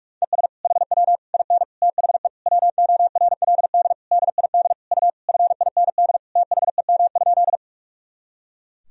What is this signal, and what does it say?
Morse code, heard as a keyed tone, about 42 wpm, 700 Hz. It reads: ES HW IN THE WORLD DID U FIND THEM?
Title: Sentences from Top 200 Words and Abbreviations 45wpm